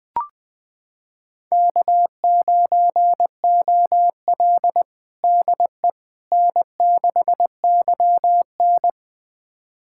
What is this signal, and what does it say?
Morse code, heard as a keyed tone, about 21 wpm, 700 Hz. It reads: K9OL DE N6YN
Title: USA Callsign DE Callsign - 4 Characters - 20wpm